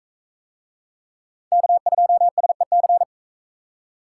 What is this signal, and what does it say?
Morse code, heard as a keyed tone, about 40 wpm, 700 Hz. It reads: K2SEC